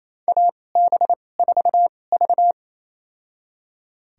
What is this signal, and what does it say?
Morse code, heard as a keyed tone, about 28 wpm, spaced as for 20 wpm, 700 Hz. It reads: AB4V
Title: USA Call Signs - 2x Character Spacing - 28wpm